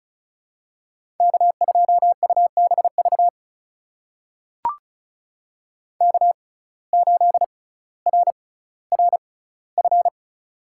Morse code, heard as a keyed tone, about 36 wpm, spaced as for 18 wpm, 700 Hz. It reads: K2UBV K8RRF
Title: USA Call Signs - 5 Characters - 6x Character Spacing - 35wpm